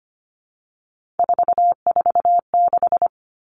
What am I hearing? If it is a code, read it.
Morse code, heard as a keyed tone, about 25 wpm, 700 Hz. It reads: 446
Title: Numbers - 3-Digit 25wpm